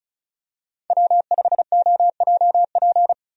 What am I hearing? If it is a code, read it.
Morse code, heard as a keyed tone, about 36 wpm, 700 Hz. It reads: W5OJP
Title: USA Call Signs - 5 Characters - 35wpm